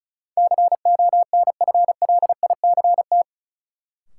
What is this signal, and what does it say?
Morse code, heard as a keyed tone, about 34 wpm, 700 Hz. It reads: CONFLICT
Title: Sentences from Top 1300 Words - New Word Drill Alt - 35wpm